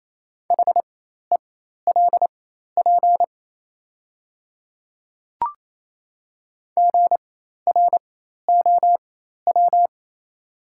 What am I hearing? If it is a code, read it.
Morse code, heard as a keyed tone, about 28 wpm, spaced as for 13 wpm, 700 Hz. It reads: HELP GROW
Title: 4x Character Spacing - Top 300 Words - 28wpm